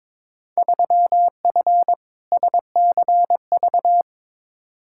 Morse code, heard as a keyed tone, about 22 wpm, 700 Hz. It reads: 3F SCV
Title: Field Day Exchanges - 22wpm